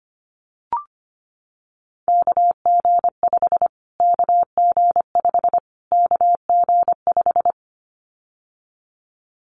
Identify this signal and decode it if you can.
Morse code, heard as a keyed tone, about 25 wpm, 700 Hz. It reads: KG5 KG5 KG5
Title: Callsign Prefixes for Top 25 DXCC - Repeated 3x - 25wpm